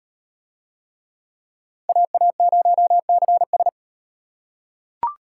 Morse code, heard as a keyed tone, about 38 wpm, 700 Hz. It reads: AA0CS